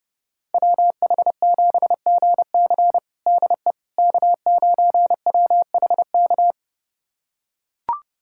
Morse code, heard as a keyed tone, about 31 wpm, 700 Hz. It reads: WH7GC DE K9WHK
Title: USA Callsign DE Callsign - 5 Characters - 30wpm